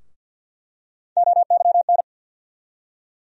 Morse code, heard as a keyed tone, about 47 wpm, 700 Hz. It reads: OYN
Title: USA Call Sign Postfixes - 3 Letters 50wpm